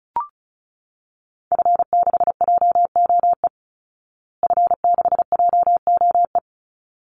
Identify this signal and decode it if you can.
Morse code, heard as a keyed tone, about 34 wpm, 700 Hz. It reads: F6JOE F6JOE
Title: Call Signs Repeated 3x - 35wpm